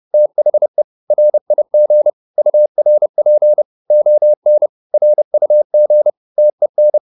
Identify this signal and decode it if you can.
Morse code, heard as a keyed tone, about 30 wpm, 600 Hz. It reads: THE RIG URP ON RUG TEN
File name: PR204 30